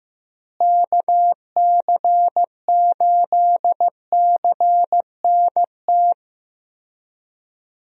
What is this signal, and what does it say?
Morse code, heard as a keyed tone, about 15 wpm, 700 Hz. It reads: KC8CNT
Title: USA Call Signs - 6 Characters - Repeated 2x - 15wpm